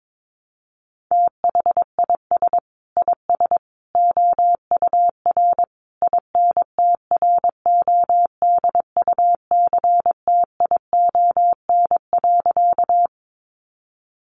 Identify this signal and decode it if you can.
Morse code, heard as a keyed tone, about 22 wpm, 700 Hz. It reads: THIS IS OUR INTRODUCTION.
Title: Sentences from Top 1000 Words 22wpm